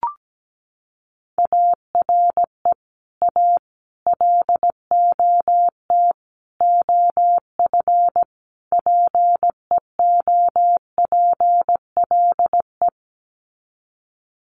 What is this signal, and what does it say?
Morse code, heard as a keyed tone, about 17 wpm, 700 Hz. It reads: ARE A LOT OF PEOPLE
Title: Sets of 5 Words 17wpm